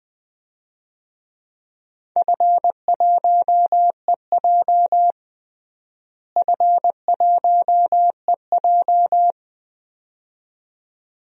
Morse code, heard as a keyed tone, about 21 wpm, 700 Hz. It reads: F1EJ F1EJ
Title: DXCC-100 Call Signs - 4 Characters - Repeated 2x - 20wpm